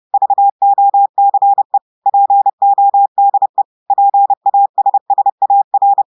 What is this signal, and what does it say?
Morse code, heard as a keyed tone, about 30 wpm, 800 Hz. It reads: VOCE PODE PASSAR
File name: audios / portugais / portugais texte en clair 2 vitesse 30 mots minutes